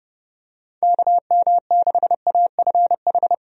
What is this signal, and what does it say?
Morse code, heard as a keyed tone, about 30 wpm, 700 Hz. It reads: KM6AFH
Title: USA Call Signs - 6 Characters - Repeated 2x - 30wpm